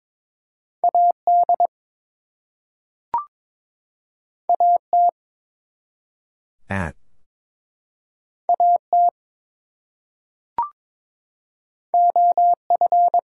Morse code, heard as a keyed tone, about 22 wpm, 700 Hz. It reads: AD AT AT OF
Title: Two Letter Words - 22wpm